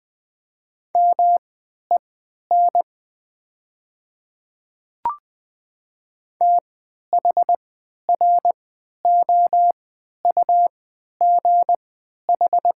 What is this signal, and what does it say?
Morse code, heard as a keyed tone, about 20 wpm, spaced as for 11 wpm, 700 Hz. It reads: MEN THROUGH
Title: 3x Character Spacing - Top 300 Words - 20wpm